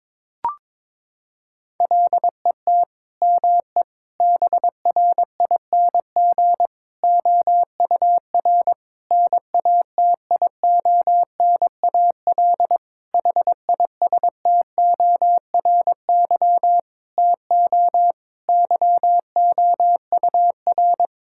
Morse code, heard as a keyed tone, about 21 wpm, 700 Hz. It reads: LET ME BRING OUR NATIONAL HISTORY TO YOUR
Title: Sentences from Top 1000 Words - Review Part 6 - 22wpm